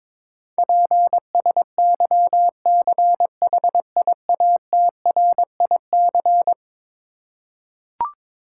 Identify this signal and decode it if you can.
Morse code, heard as a keyed tone, about 21 wpm, 700 Hz. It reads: PSYCHIATRIC
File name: practice / 22wpm / Eleven Letter Words - 22wpm